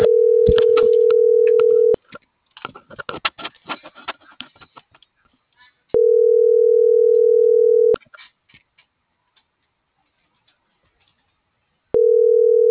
Ringback tone (North America), 440 + 480 Hz.